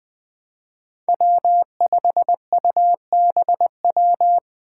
Morse code, heard as a keyed tone, about 21 wpm, 700 Hz. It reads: W5UBW
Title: Five Challenging Letter-Number Combinations 20wpm